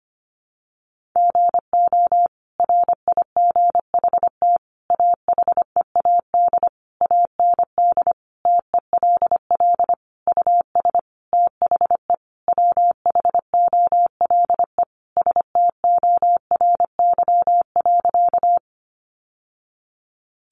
Morse code, heard as a keyed tone, about 25 wpm, 700 Hz. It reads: GO RIGHT AHEAD AND TELL US THE WHOLE STORY.
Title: Sentences from Top 1100 Words 25wpm